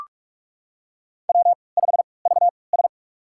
Morse code, heard as a keyed tone, about 44 wpm, spaced as for 25 wpm, 700 Hz. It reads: W5VS
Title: USA Call Signs - 4 Characters - 3x Character Spacing - 45wpm